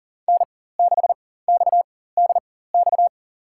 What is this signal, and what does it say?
Morse code, heard as a keyed tone, about 40 wpm, spaced as for 19 wpm, 700 Hz. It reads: N6XDX
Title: USA Call Signs - 5 Characters - 4x Character Spacing - 40wpm